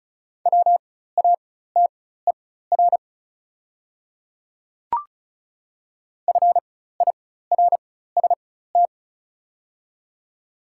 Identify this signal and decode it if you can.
Morse code, heard as a keyed tone, about 36 wpm, spaced as for 16 wpm, 700 Hz. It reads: WATER FIRST
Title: 4x Character Spacing - Top 100 Words - 35wpm